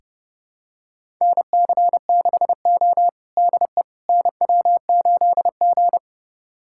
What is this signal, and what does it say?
Morse code, heard as a keyed tone, about 30 wpm, 700 Hz. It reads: NC6O DE NW8G
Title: USA Callsign DE Callsign - 4 Characters - 30wpm